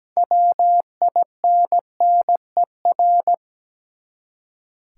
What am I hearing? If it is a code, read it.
Morse code, heard as a keyed tone, about 17 wpm, 700 Hz. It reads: WINNER
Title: Six Letter Words - 17wpm